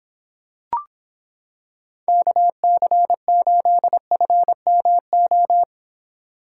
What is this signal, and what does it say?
Morse code, heard as a keyed tone, about 26 wpm, 700 Hz. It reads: KC8FMO